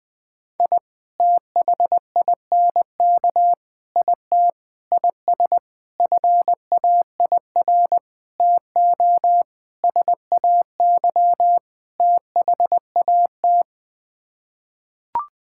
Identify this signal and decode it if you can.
Morse code, heard as a keyed tone, about 21 wpm, 700 Hz. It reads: I THINK IT IS FAIR TO SAY THAT
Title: Sets of 8 Words - 20wpm